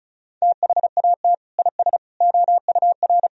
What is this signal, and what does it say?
Morse code, heard as a keyed tone, about 36 wpm, 700 Hz. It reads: THAT IS OUR
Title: Sentences from Top 1000 Words - Review Part 4 - 35wpm